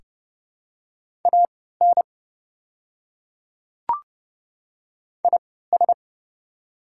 Morse code, heard as a keyed tone, about 30 wpm, spaced as for 17 wpm, 700 Hz. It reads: AN IS
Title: 3x Character Spacing - Top 100 Words - 30wpm